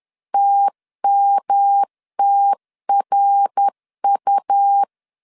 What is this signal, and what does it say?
Morse code, heard as a keyed tone, about 10 wpm, 800 Hz. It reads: TMTRU
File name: CW / DET / 1 3 1500C S C 10wpm 3335